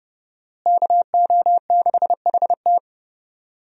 Morse code, heard as a keyed tone, about 31 wpm, 700 Hz. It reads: KO6HT
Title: USA Call Signs - 5 Characters - Repeated 2x - 30wpm